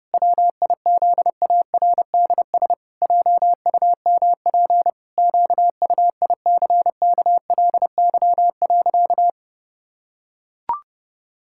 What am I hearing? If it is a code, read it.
Morse code, heard as a keyed tone, about 30 wpm, 700 Hz. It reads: WIZARDS JUMP QUICKLY.
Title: Pangram Challenge v2 - 30wpm